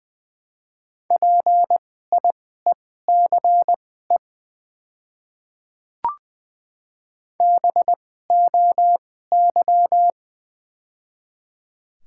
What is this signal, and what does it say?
Morse code, heard as a keyed tone, about 20 wpm, spaced as for 14 wpm, 700 Hz. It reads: PIECE BOY
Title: 2x Character Spacing - Top 500 Words - 20wpm